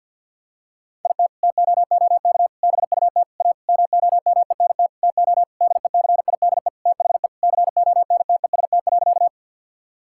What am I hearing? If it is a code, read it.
Morse code, heard as a keyed tone, about 50 wpm, 700 Hz. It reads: IT TOOK BUT A MOMENT TO DECIDE THE CONTEST.
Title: Sentences from Top 1600 Words 50wpm